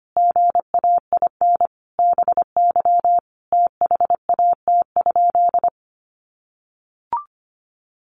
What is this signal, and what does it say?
Morse code, heard as a keyed tone, about 25 wpm, 700 Hz. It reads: GAIN BY THAT?
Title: Sentences from Top 1100 Words 25wpm